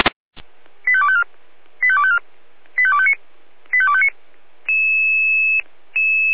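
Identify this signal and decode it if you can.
Five-tone selective call (ZVEI), heard as quick runs of five tones, tones of about 70 ms. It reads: 85254 85254 85259 85259
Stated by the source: ZVEI five-tone selective call received on SDR.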